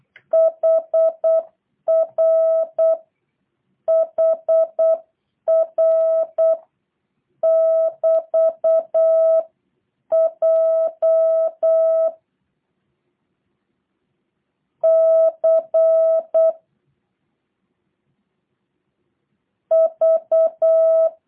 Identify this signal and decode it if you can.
Morse code, heard as a keyed tone, about 8 wpm, 650 Hz. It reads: HR HR <BT> J C V
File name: jh8cbh / 2023cw / review1-5